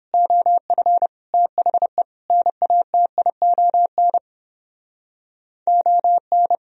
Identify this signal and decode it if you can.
Morse code, heard as a keyed tone, about 29 wpm, 700 Hz. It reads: OF THE NATION ON